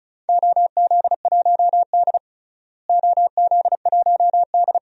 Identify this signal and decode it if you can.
Morse code, heard as a keyed tone, about 34 wpm, 700 Hz. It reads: OZ1D OZ1D
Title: DXCC-100 Call Signs - 4 Characters - Repeated 2x - 35wpm